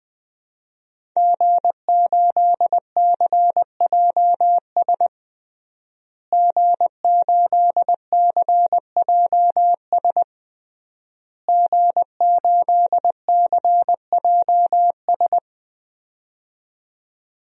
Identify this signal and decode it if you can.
Morse code, heard as a keyed tone, about 20 wpm, 700 Hz. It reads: G8CJS G8CJS G8CJS
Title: DXCC-100 Call Signs - 5 Characters - Repeated 3x - 20wpm